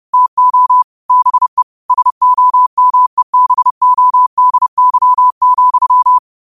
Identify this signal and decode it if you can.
Morse code, heard as a keyed tone, about 30 wpm, 1000 Hz. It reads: TO BE SOMEBODY,